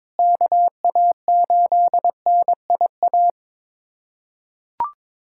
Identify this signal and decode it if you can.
Morse code, heard as a keyed tone, about 22 wpm, 700 Hz. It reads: KA8NIA
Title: USA Callsign DE Callsign - 6 Characters - 22wpm